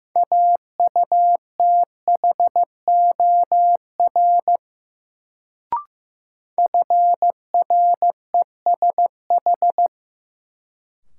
Morse code, heard as a keyed tone, about 15 wpm, 700 Hz. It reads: AUTHOR FRESH
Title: Sentences from Top 1200 Words - New Word Drill - 15wpm